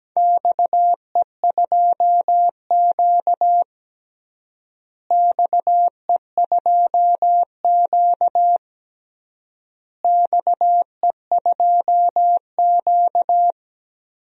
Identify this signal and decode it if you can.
Morse code, heard as a keyed tone, about 17 wpm, 700 Hz. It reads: XE2Q XE2Q XE2Q
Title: DXCC-100 Call Signs - 4 Characters - Repeated 3x - 17wpm